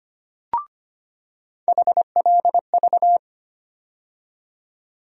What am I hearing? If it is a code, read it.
Morse code, heard as a keyed tone, about 25 wpm, 700 Hz. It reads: HLV
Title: USA Call Sign Postfixes - 3 Letters 25wpm